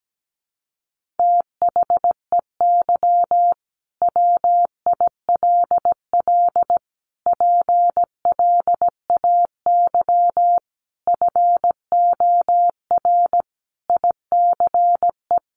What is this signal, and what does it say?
Morse code, heard as a keyed tone, about 17 wpm, 700 Hz. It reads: THEY WILL PLAY FOR ICE
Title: Sentences from Top 1500 Words 17wpm